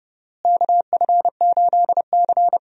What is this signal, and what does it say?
Morse code, heard as a keyed tone, about 30 wpm, 700 Hz. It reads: KF8C
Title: USA Call Signs - 4 Characters - Repeated 3x - 30wpm